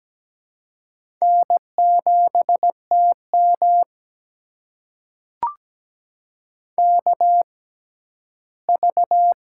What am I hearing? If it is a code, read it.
Morse code, heard as a keyed tone, about 17 wpm, spaced as for 11 wpm, 700 Hz. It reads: N7TM KV
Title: USA Call Signs - 4 Characters - 6x Character Spacing - 17wpm